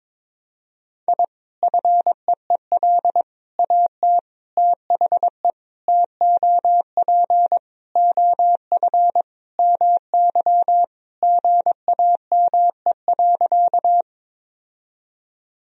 Morse code, heard as a keyed tone, about 21 wpm, 700 Hz. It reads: I FEEL AT THE TOP OF MY GAME.
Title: Sentences from Top 1000 Words - Review Part 2 - 22wpm